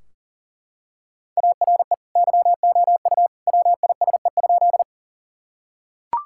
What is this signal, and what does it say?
Morse code, heard as a keyed tone, about 40 wpm, 700 Hz. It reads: ARE YOU WISE?
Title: Sentences from Top 1600 Words 40wpm